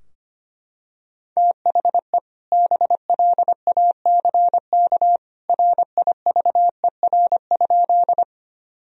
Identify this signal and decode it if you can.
Morse code, heard as a keyed tone, about 25 wpm, 700 Hz. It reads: THE BLACK RIVER?
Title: Sentences from Top 1000 Words - Review Part 2 - 25wpm